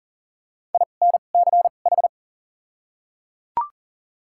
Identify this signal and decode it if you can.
Morse code, heard as a keyed tone, about 40 wpm, 700 Hz. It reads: I N C H
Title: 2x Character Spacing - Top 500 Words - 40wpm